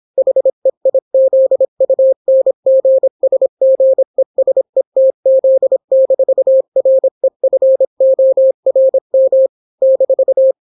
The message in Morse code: HEIZUNGSGESETZ-REFORM -